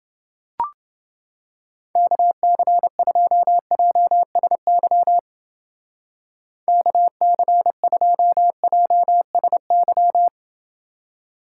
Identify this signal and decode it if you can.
Morse code, heard as a keyed tone, about 28 wpm, 700 Hz. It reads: KC2JSY KC2JSY